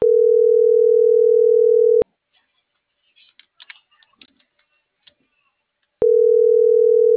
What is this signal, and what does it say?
Ringback tone (North America), 440 + 480 Hz.